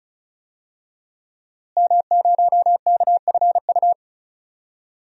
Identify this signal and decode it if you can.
Morse code, heard as a keyed tone, about 34 wpm, 700 Hz. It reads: M0KFU
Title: DXCC-100 Call Signs - 5 Characters - Repeated 2x - 35wpm